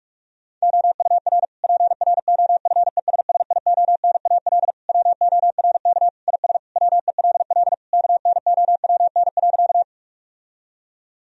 Morse code, heard as a keyed tone, about 41 wpm, 700 Hz. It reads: OUR PROFESSIONAL WORK IS WELL KNOWN.
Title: Sentences from Top 1000 Words - Review Part 4 - 45wpm